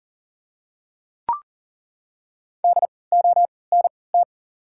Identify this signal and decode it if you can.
Morse code, heard as a keyed tone, about 40 wpm, spaced as for 23 wpm, 700 Hz. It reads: DONT
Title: 3x Character Spacing - Top 300 Words - 40wpm